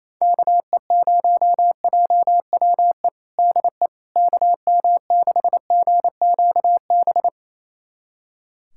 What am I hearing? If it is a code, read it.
Morse code, heard as a keyed tone, about 28 wpm, 700 Hz. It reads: KE0JWE DE KM6GQB
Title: USA Callsign DE Callsign 28wpm